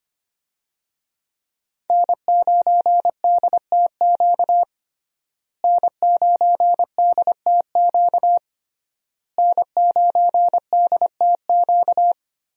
Morse code, heard as a keyed tone, about 25 wpm, 700 Hz. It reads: N9DTQ N9DTQ N9DTQ
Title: USA Call Signs - 5 Characters - Repeated 3x - 25wpm